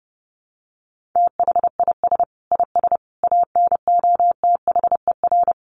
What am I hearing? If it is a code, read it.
Morse code, heard as a keyed tone, about 30 wpm, 700 Hz. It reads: THIS IS ANOTHER
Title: Sentences from Top 1400 Words 30wpm